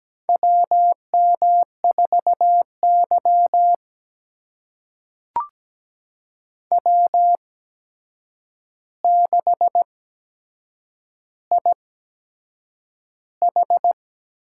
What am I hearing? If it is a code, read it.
Morse code, heard as a keyed tone, about 17 wpm, spaced as for 7 wpm, 700 Hz. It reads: WM4Y W6IH
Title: USA Call Signs - 8x Character Spacing - 17wpm